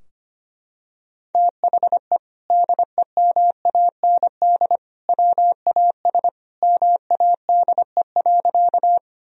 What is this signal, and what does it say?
Morse code, heard as a keyed tone, about 25 wpm, 700 Hz. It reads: THE DEMAND WAS MADE.
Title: Sentences from Top 1000 Words - Review Part 4 - 25wpm